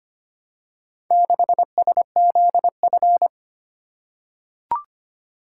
Morse code, heard as a keyed tone, about 25 wpm, 700 Hz. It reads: 6SZF
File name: practice / 25wpm / Four Letter-Number Combinations - 25wpm